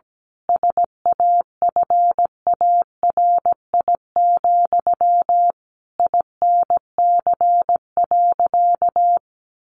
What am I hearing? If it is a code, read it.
Morse code, heard as a keyed tone, about 17 wpm, 700 Hz. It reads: SAFARI, INC.
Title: A Sound Of Thunder 17wpm